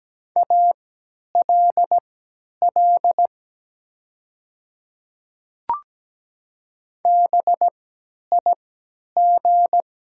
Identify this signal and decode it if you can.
Morse code, heard as a keyed tone, about 17 wpm, spaced as for 10 wpm, 700 Hz. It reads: ALL BIG
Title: 3x Character Spacing - Top 200 Words - 17wpm